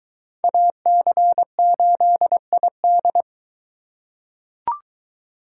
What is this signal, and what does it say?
Morse code, heard as a keyed tone, about 22 wpm, 700 Hz. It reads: AC8ID